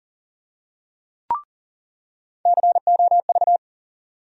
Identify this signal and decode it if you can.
Morse code, heard as a keyed tone, about 40 wpm, 700 Hz. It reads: COV
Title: USA Call Sign Postfixes - 3 Letters 40wpm